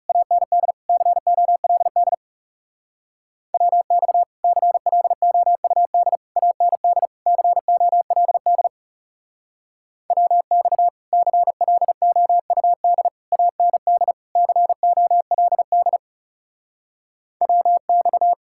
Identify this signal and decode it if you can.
Morse code, heard as a keyed tone, about 37 wpm, 700 Hz. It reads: AND COLD WX CLOUD AND COLD WX CLOUD AND COLD WX
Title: Speed-Racing Common QSO Phrases - 30wpm